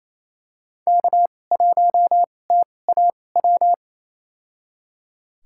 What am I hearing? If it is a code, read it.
Morse code, heard as a keyed tone, about 28 wpm, spaced as for 20 wpm, 700 Hz. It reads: K1TAW
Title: USA Call Signs - 5 Characters - 2x Character Spacing - 28wpm